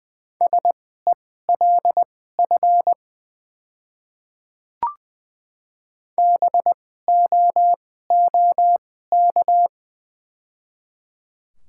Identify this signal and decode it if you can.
Morse code, heard as a keyed tone, about 21 wpm, spaced as for 15 wpm, 690 Hz. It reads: SELF BOOK
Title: 2x Character Spacing - Top 300 Words - 20wpm